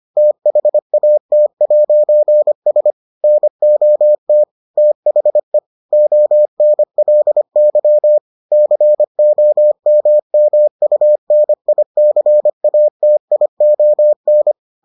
Morse code, heard as a keyed tone, about 25 wpm, 600 Hz. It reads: THAT'S NOT THE ONLY COMMUNICATION